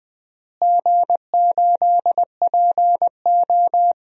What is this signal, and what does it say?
Morse code, heard as a keyed tone, about 20 wpm, 700 Hz. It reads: G8PO
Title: DXCC-100 Call Signs - 4 Characters - Repeated 3x - 20wpm